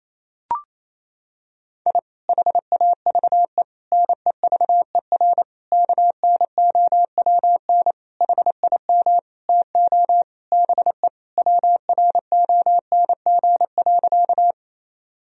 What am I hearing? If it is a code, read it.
Morse code, heard as a keyed tone, about 28 wpm, 700 Hz. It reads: I HAVE NEVER KNOWN HIM TO BE WRONG.
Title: Sentences from Top 1000 Words - Review Part 3 - 28wpm